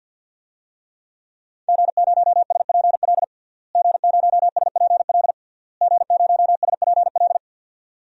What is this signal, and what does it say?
Morse code, heard as a keyed tone, about 52 wpm, 700 Hz. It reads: G0SPL G0SPL G0SPL
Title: DXCC-100 Call Signs - 5 Characters - Repeated 3x - 50wpm